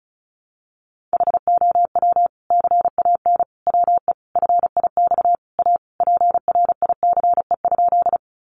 Morse code, heard as a keyed tone, about 34 wpm, 700 Hz. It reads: HOW CAN WE FIX A PRICE?
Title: Sentences from Top 1400 Words 35wpm